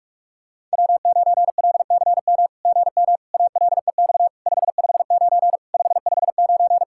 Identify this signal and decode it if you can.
Morse code, heard as a keyed tone, about 44 wpm, 690 Hz. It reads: W9LCM GM ALEX 559 559
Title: SOTA QSO Practice 45wpm